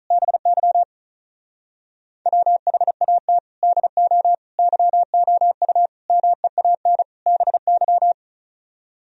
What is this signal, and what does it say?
Morse code, heard as a keyed tone, about 45 wpm, 700 Hz. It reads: BY WHAT DO YOU MEAN BY